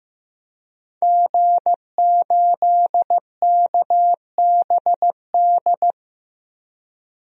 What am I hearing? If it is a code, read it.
Morse code, heard as a keyed tone, about 15 wpm, 700 Hz. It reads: G8KBD
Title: DXCC-100 Call Signs - 5 Characters - Repeated 2x - 15wpm